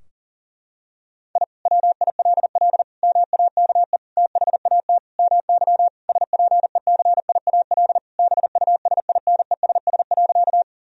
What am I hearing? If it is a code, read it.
Morse code, heard as a keyed tone, about 37 wpm, 700 Hz. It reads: I WILL MAKE THAT MY SPECIAL BUSINESS.
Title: Sentences from Top 1000 Words - Review Part 3 - 40wpm